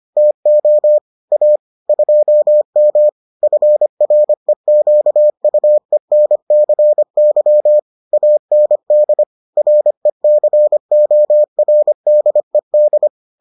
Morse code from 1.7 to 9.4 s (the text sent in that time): 2M FREQUENCY AND